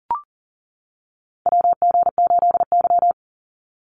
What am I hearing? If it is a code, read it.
Morse code, heard as a keyed tone, about 38 wpm, 700 Hz. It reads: WG8Y
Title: NA SOTA Chaser Callsigns 40wpm